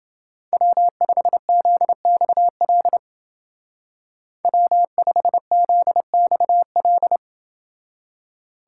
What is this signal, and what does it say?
Morse code, heard as a keyed tone, about 29 wpm, 700 Hz. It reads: W5ZXL W5ZXL